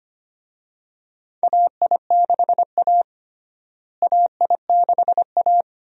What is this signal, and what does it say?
Morse code, heard as a keyed tone, about 25 wpm, 700 Hz. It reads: AI6A AI6A
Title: USA Call Signs - 4 Characters - Repeated 2x - 25wpm